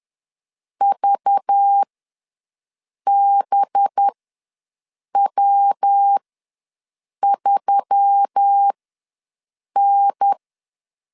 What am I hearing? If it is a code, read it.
Morse code, heard as a keyed tone, about 10 wpm, spaced as for 6 wpm, 800 Hz. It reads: VBW3N
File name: LEZIONI / 8 YJ45 / 8 YJ4 ws 10wpm 4020